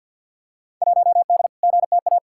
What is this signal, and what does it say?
Morse code, heard as a keyed tone, about 47 wpm, 700 Hz. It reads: 1D GTA
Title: Field Day Exchanges - 50wpm